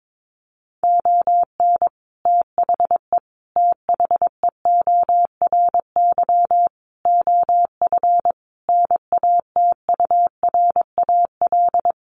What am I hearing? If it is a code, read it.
Morse code, heard as a keyed tone, about 22 wpm, 700 Hz. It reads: ON THE THEORY OF NATURAL
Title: Sentences from Top 1400 Words 22wpm